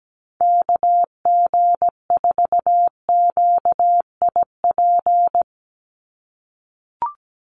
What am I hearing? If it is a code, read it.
Morse code, heard as a keyed tone, about 17 wpm, 700 Hz. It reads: KG4QIP
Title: USA Call Signs 17wpm